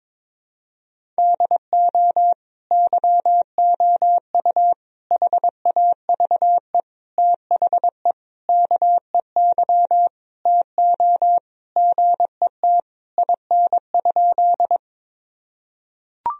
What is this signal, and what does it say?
Morse code, heard as a keyed tone, about 21 wpm, 700 Hz. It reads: DO YOU HAVE THE KEY TO GET IN?
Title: Sentences from Top 1000 Words - Review Part 4 - 22wpm